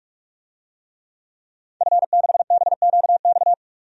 Morse code, heard as a keyed tone, about 45 wpm, 700 Hz. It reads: F6BQX
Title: DXCC-100 Call Signs - 5 Characters - 45wpm